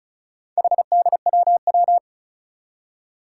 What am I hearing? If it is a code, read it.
Morse code, heard as a keyed tone, about 36 wpm, 700 Hz. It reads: HDWW
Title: Four Challenging Letter-Number Combinations 35wpm